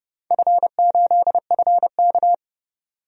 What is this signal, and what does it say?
Morse code, heard as a keyed tone, about 30 wpm, 700 Hz. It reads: F8FK
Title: DXCC-100 Call Signs - 4 Characters - 30wpm